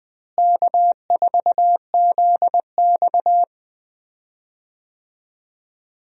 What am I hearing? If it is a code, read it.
Morse code, heard as a keyed tone, about 20 wpm, 700 Hz. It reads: K4ZX
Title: USA Call Signs - 4 Characters - Repeated 3x - 20wpm